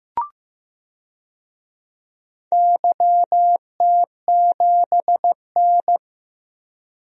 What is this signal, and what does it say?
Morse code, heard as a keyed tone, about 15 wpm, 700 Hz. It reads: YT7N
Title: DXCC-100 Call Signs - 4 Characters - 15wpm